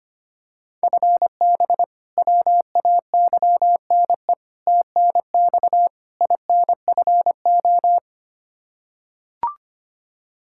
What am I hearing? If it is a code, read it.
Morse code, heard as a keyed tone, about 25 wpm, 700 Hz. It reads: FB WAYNE TNX INFO
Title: Practice QSO - Exchange 3 - More Info - 25wpm